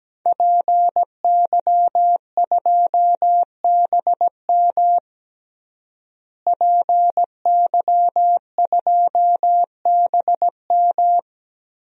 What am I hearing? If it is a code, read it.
Morse code, heard as a keyed tone, about 17 wpm, 700 Hz. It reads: PY2BM PY2BM
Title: DXCC-100 Call Signs - 5 Characters - Repeated 3x - 17wpm